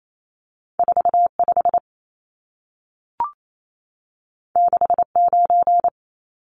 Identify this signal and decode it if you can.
Morse code, heard as a keyed tone, about 28 wpm, 700 Hz. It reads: 45 69
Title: Numbers - 2-Digit 28wpm